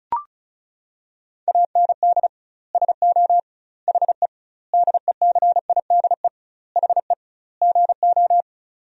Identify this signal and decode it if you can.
Morse code, heard as a keyed tone, about 36 wpm, 700 Hz. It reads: AND SO HE DECIDE HE GO
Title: 2x Word-Spacing - Sentences from Top 500 Words - 35wpm